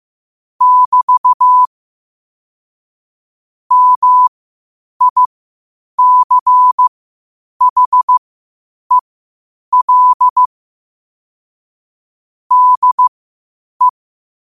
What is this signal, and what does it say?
Morse code, heard as a keyed tone, about 15 wpm, spaced as for 8 wpm, 1000 Hz. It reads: <BT> MICHEL DE
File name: qotdfiles / 8wpm